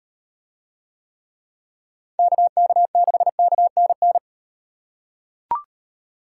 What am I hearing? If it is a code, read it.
Morse code, heard as a keyed tone, about 38 wpm, 700 Hz. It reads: KK6KNN